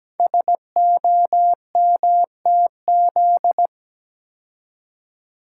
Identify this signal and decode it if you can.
Morse code, heard as a keyed tone, about 17 wpm, 700 Hz. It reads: SOMTZ
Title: Code Groups - 5 Letter:Numbers - 17wpm